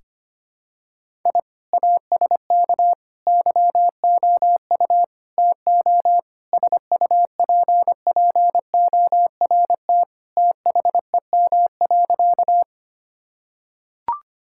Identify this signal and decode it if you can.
Morse code, heard as a keyed tone, about 25 wpm, 700 Hz. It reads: I ASK YOU TO SUPPORT THEM.
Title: Sentences from Top 1000 Words - Review Part 5 - 25wpm